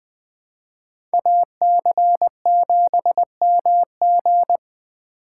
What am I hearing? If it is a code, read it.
Morse code, heard as a keyed tone, about 20 wpm, 700 Hz. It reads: AC7MG
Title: USA Call Signs - 2x Character Spacing - 20wpm